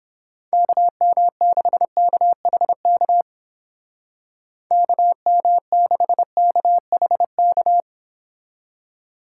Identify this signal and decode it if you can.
Morse code, heard as a keyed tone, about 30 wpm, 700 Hz. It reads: KM6KHK KM6KHK